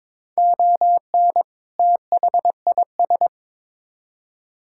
Morse code, heard as a keyed tone, about 22 wpm, 700 Hz. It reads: ON THIS
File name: Build A Phrase - 2-Word Phrases - 22wpm